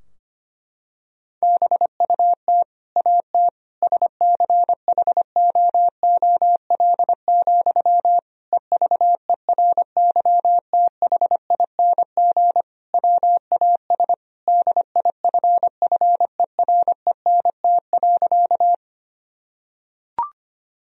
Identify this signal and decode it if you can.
Morse code, heard as a keyed tone, about 25 wpm, 700 Hz. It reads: BUT AT SCHOOL, EVERYTHING WAS DIFFERENT.
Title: Sentences from Top 1000 Words - Review Part 3 - 25wpm